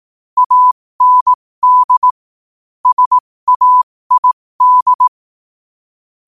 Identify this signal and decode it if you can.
Morse code, heard as a keyed tone, about 18 wpm, spaced as for 15 wpm, 1000 Hz. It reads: AND SAID
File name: qotdfiles / 15wpm